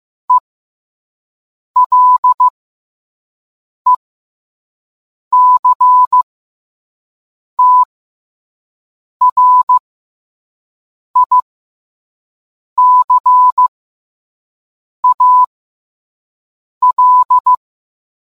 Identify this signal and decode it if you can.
Morse code, heard as a keyed tone, about 15 wpm, spaced as for 5 wpm, 1000 Hz. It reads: ELECTRICAL